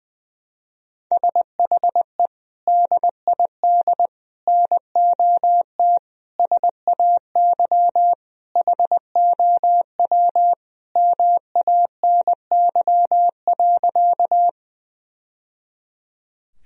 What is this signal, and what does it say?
Morse code, heard as a keyed tone, about 21 wpm, 700 Hz. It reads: SHE DID NOT SAY HOW MANY.
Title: Sentences from Top 1000 Words - Review Part 1 - 20wpm